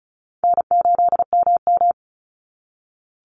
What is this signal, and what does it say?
Morse code, heard as a keyed tone, about 36 wpm, 700 Hz. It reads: N8MM
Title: USA Call Signs 35wpm